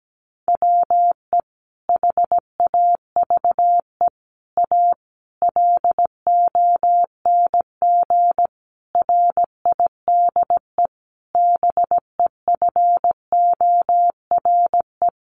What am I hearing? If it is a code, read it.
Morse code, heard as a keyed tone, about 17 wpm, 700 Hz. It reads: WE HAVE A LONG RIDE BEFORE
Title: Sentences from Top 1200 Words 17wpm